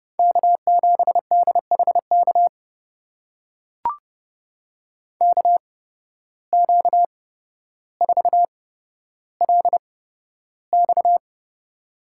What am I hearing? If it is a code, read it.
Morse code, heard as a keyed tone, about 30 wpm, spaced as for 13 wpm, 700 Hz. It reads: K7DHK KQ4LX
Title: USA Call Signs - 5 Characters - 8x Character Spacing - 30wpm